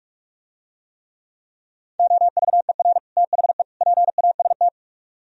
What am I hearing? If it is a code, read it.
Morse code, heard as a keyed tone, about 48 wpm, 700 Hz. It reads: OVER THE PAST